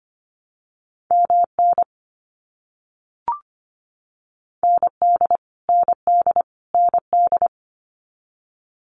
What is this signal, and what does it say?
Morse code, heard as a keyed tone, about 25 wpm, 700 Hz. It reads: MN ND ND ND
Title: US States Repeated 3x 25wpm